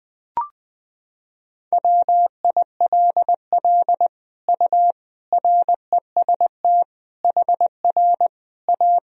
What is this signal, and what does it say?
Morse code, heard as a keyed tone, about 20 wpm, 700 Hz. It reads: WILL U REST HR A
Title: Sentences from Top 500 Words and Abbreviations 20wpm